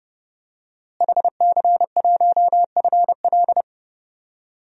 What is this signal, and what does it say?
Morse code, heard as a keyed tone, about 32 wpm, 700 Hz. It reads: HC1FL
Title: DXCC-100 Call Signs - 5 Characters - Repeated 2x - 30wpm